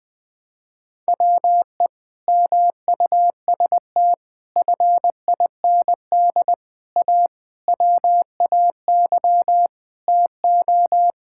Morse code, heard as a keyed tone, about 20 wpm, 700 Hz. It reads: WE MUST FIND A WAY TO
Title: Sentences with 4-Word Phrases - 20wpm